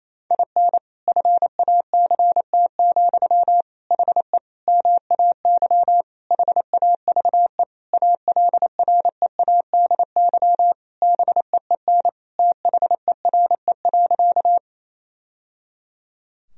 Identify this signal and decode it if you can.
Morse code, heard as a keyed tone, about 28 wpm, 700 Hz. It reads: IN FACT, HE MAY HAVE ALREADY BEEN THERE.
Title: Sentences from Top 1000 Words - Review Part 3 - 28wpm